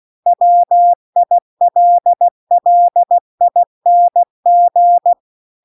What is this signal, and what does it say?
Morse code, heard as a keyed tone, about 16 wpm, 700 Hz. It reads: WILLING